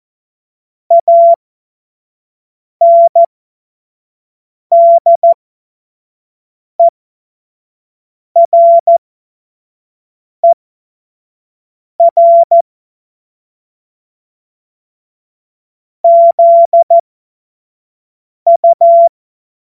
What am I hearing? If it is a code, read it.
Morse code, heard as a keyed tone, about 14 wpm, spaced as for 5 wpm, 680 Hz. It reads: ANDERER ZU
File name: Morsekurs / Witze / Witz 04. April 2023 25 F70